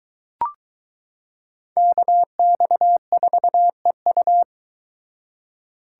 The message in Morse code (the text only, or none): KX4EU